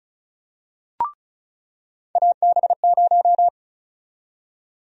Morse code, heard as a keyed tone, about 36 wpm, 700 Hz. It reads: AB0
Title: USA Call Sign Prefixes - 35wpm